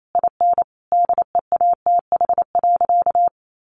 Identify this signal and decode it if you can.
Morse code, heard as a keyed tone, about 28 wpm, 700 Hz. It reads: IN DEATH.
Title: Sentences from Top 1300 Words 28wpm